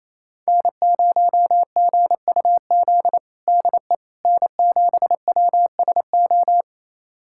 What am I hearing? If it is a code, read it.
Morse code, heard as a keyed tone, about 28 wpm, 700 Hz. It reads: N0GUZ DE N7WSO
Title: USA Callsign DE Callsign - 5 Characters - 28wpm